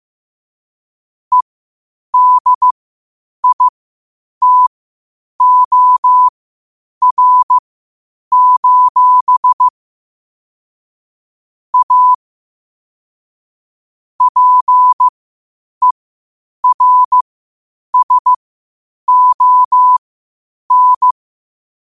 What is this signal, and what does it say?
Morse code, heard as a keyed tone, about 15 wpm, spaced as for 8 wpm, 1000 Hz. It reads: EDITOR: A PERSON
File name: qotdfiles / 8wpm